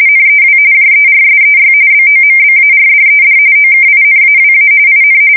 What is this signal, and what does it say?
RTTY, about 45.45 baud. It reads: NEW JERSEY BOMBARDED OKINAWA IN
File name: fsk170 / USS-New-Jersey